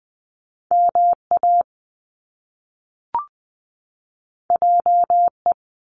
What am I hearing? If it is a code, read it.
Morse code, heard as a keyed tone, about 20 wpm, 700 Hz. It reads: MA JE
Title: Top Two Letter Combinations in Words 20wpm